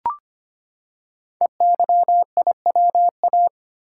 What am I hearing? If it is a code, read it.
Morse code, heard as a keyed tone, about 25 wpm, 700 Hz. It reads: EYIWA
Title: Code Groups - 5 Letters - 25wpm